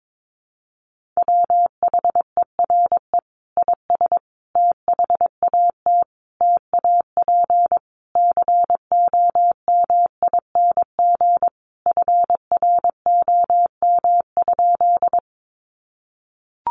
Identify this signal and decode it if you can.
Morse code, heard as a keyed tone, about 22 wpm, 700 Hz. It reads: WHERE IS THAT TAP COMING FROM?
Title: Sentences from Top 1400 Words 22wpm